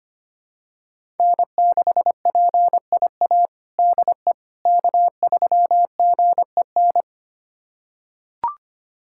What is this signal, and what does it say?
Morse code, heard as a keyed tone, about 25 wpm, 700 Hz. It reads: N6PIA DE K3GEN
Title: USA Callsign DE Callsign - 5 Characters - 25wpm